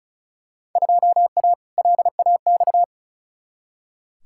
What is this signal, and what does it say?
Morse code, heard as a keyed tone, about 36 wpm, 700 Hz. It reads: 2A LAX
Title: Field Day Exchanges - 35wpm